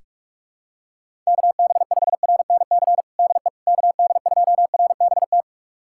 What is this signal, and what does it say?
Morse code, heard as a keyed tone, about 46 wpm, 700 Hz. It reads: KB5RNC DE KD2RBT
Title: USA Callsign DE Callsign - 6 Characters - 45wpm